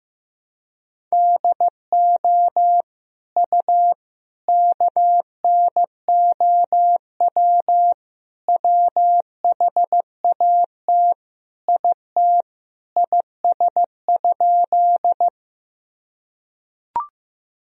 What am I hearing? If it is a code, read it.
Morse code, heard as a keyed tone, about 15 wpm, 700 Hz. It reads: DO U KNOW WHAT IT IS?
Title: Sentences from Top 100 Words and Abbreviations 15wpm